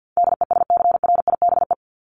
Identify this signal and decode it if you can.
Morse code, heard as a keyed tone, about 50 wpm, 700 Hz. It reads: DESCRIBE
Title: ED Suffix - Top 100 Words with Base Words 50wpm